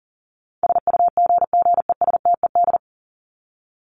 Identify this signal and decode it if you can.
Morse code, heard as a keyed tone, about 40 wpm, 700 Hz. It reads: SUGGESTED
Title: ED Suffix - Top 100 Words with Base Words 40wpm